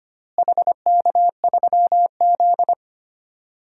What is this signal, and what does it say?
Morse code, heard as a keyed tone, about 25 wpm, 700 Hz. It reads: HK3Z
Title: DXCC-100 Call Signs - 4 Characters - Repeated 3x - 25wpm